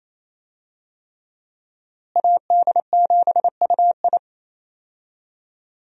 Morse code, heard as a keyed tone, about 28 wpm, 700 Hz. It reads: AD7UI
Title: USA Call Signs - 5 Characters - Repeated 3x - 28wpm